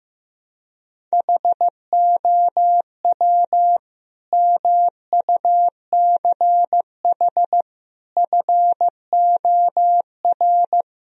Morse code, heard as a keyed tone, about 15 wpm, 700 Hz. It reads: HOW MUCH FOR
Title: Sentences from Top 1000 Words - Review Part 3 - 15wpm